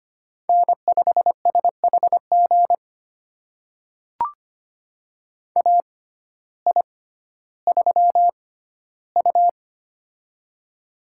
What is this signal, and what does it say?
Morse code, heard as a keyed tone, about 25 wpm, spaced as for 14 wpm, 700 Hz. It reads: N5SHG AI3U
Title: USA Call Signs - 6x Character Spacing - 25wpm